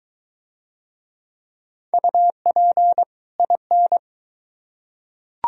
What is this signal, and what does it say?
Morse code, heard as a keyed tone, about 22 wpm, 700 Hz. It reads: UP IN